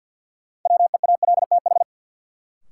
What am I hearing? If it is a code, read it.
Morse code, heard as a keyed tone, about 51 wpm, 700 Hz. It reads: WEALTH
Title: Sentences from Top 1000 Words - Review Part 4 - New Word Drill - 50wpm